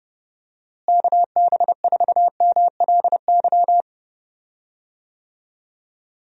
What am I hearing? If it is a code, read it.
Morse code, heard as a keyed tone, about 30 wpm, 700 Hz. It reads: KB4MLY
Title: USA Call Signs - 6 Characters - Repeated 2x - 30wpm